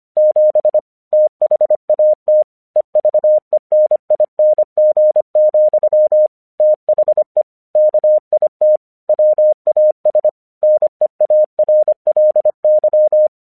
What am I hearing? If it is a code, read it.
Morse code, heard as a keyed tone, about 26 wpm, 610 Hz. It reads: 7 THAT EVENING, THE KIT WAS NEARLY
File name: 1 English 25wpm / AQC27